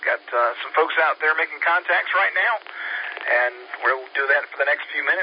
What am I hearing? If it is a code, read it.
Emergency Alert System (SAME) tones, heard as harsh buzzing bursts, with speech. It is heard from 1.2 to 4.6 s.